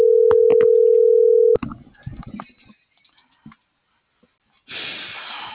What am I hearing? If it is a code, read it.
Ringback tone (North America), 440 + 480 Hz.